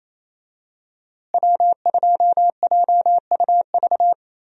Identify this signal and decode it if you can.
Morse code, heard as a keyed tone, about 28 wpm, 700 Hz. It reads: W2JUV
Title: USA Call Signs - 5 Characters - Repeated 2x - 28wpm